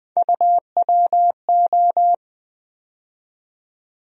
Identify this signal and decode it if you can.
Morse code, heard as a keyed tone, about 20 wpm, 700 Hz. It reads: UWO
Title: USA Call Sign Postfixes - 3 Letters 20wpm